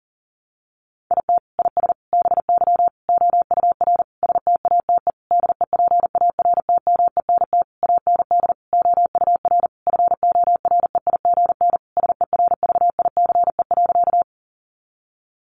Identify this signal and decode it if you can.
Morse code, heard as a keyed tone, about 40 wpm, 700 Hz. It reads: IT IS BY OUR STATE DEPARTMENT AND OUR FOREIGN SERVICE.
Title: Sentences from Top 1400 Words 40wpm